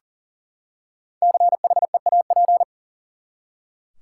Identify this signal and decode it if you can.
Morse code, heard as a keyed tone, about 38 wpm, 700 Hz. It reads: CHEAP
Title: Five Letter Words - 40wpm